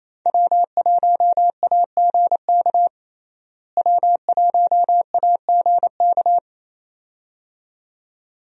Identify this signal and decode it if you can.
Morse code, heard as a keyed tone, about 28 wpm, 700 Hz. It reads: W1AGK W1AGK
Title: USA Call Signs - 5 Characters - Repeated 2x - 28wpm